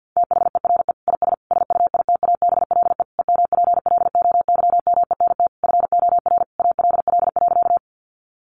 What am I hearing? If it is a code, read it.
Morse code, heard as a keyed tone, about 50 wpm, 700 Hz. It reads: THERE IS SUITABLE EMPLOYMENT FOR ALL.
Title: Sentences from Top 1500 Words 50wpm